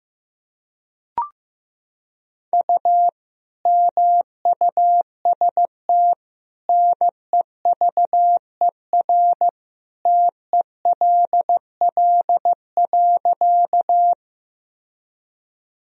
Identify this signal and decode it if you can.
Morse code, heard as a keyed tone, about 15 wpm, 700 Hz. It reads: U MUST NEVER TELL.
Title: Sentences from Top 300 Words and Abbreviations 15wpm